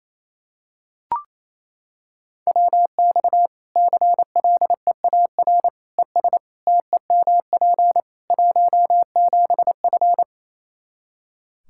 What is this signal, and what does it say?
Morse code, heard as a keyed tone, about 28 wpm, 700 Hz. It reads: WX CLEAR ES TEMP 17F
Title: Practice QSO - Exchange 2 - Info - 28wpm